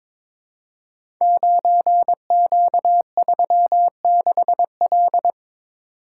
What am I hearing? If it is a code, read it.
Morse code, heard as a keyed tone, about 22 wpm, 700 Hz. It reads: 9Q36L
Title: Code Groups - 5 Letter:Numbers - 22wpm